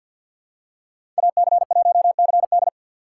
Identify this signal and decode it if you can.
Morse code, heard as a keyed tone, about 49 wpm, 700 Hz. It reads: AC1CD
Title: USA Call Signs - 5 Characters - 3x Character Spacing - 50wpm